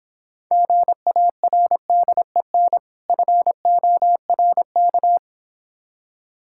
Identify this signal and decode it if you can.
Morse code, heard as a keyed tone, about 26 wpm, 700 Hz. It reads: GARDEN FORK